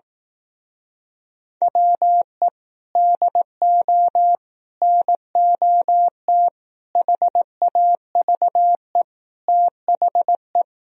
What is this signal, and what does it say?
Morse code, heard as a keyed tone, about 18 wpm, 700 Hz. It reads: WE DO NOT HAVE THE
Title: Speed-Racing - Sets of 5 Words 15wpm